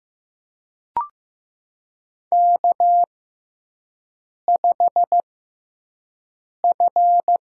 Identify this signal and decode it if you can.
Morse code, heard as a keyed tone, about 15 wpm, spaced as for 5 wpm, 700 Hz. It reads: K5F
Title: USA Call Signs - 6x Character Spacing - 15wpm